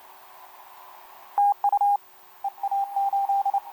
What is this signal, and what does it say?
Morse code, heard as a keyed tone, about 28 wpm, 810 Hz, with radio noise. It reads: TU EA8
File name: kp2-ks9w sm1cq etc 12m cw 13mar2011 1339